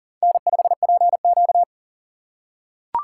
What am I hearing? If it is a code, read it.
Morse code, heard as a keyed tone, about 37 wpm, 700 Hz. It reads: N5PQ
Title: USA Call Signs - 4x Character Spacing - 40wpm